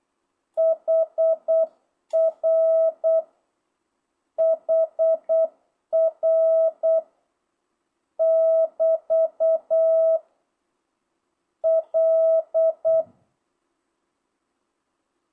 Morse code, heard as a keyed tone, about 8 wpm, 1300 Hz, with noise. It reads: HR HR <BT> L
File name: jh8cbh / 2025cw